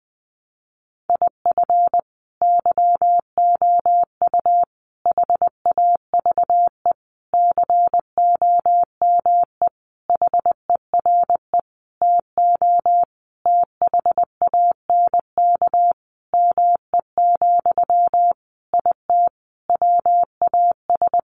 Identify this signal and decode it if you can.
Morse code, heard as a keyed tone, about 20 wpm, 700 Hz. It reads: IF YOU HAVE COME HERE TO THANK ME, IT WAS
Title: Sentences from Top 800 Words 20wpm